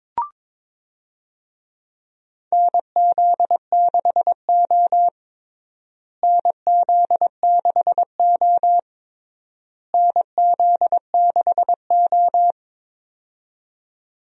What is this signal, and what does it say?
Morse code, heard as a keyed tone, about 22 wpm, 700 Hz. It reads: NZ6O NZ6O NZ6O
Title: USA Call Signs - 4 Characters - Repeated 3x - 22wpm